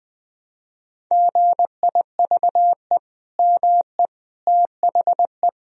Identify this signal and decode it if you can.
Morse code, heard as a keyed tone, about 21 wpm, 700 Hz. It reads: GIVE ME THE
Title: Sentences from Top 1000 Words - Review Part 4 - 20wpm